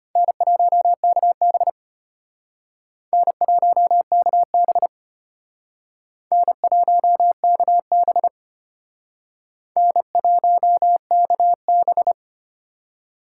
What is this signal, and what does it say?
Morse code, heard as a keyed tone, about 31 wpm, 700 Hz. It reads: N1KB N1KB N1KB N1KB